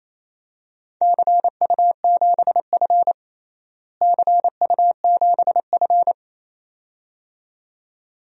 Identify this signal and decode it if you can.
Morse code, heard as a keyed tone, about 28 wpm, 700 Hz. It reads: CU7F CU7F
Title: DXCC-100 Call Signs - 4 Characters - Repeated 2x - 28wpm